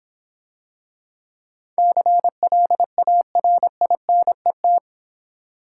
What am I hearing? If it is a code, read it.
Morse code, heard as a keyed tone, about 25 wpm, 700 Hz. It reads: CLARINET